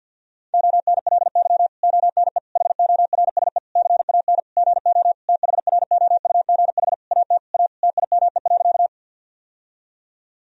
Morse code, heard as a keyed tone, about 52 wpm, 700 Hz. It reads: ONLY ONE HORSE CAN GO THROUGH AT A TIME.
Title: Sentences from Top 1000 Words - Review Part 1 - 50wpm